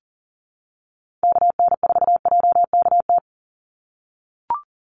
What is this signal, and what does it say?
Morse code, heard as a keyed tone, about 40 wpm, 700 Hz. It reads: KN4JKT
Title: USA Call Signs 40wpm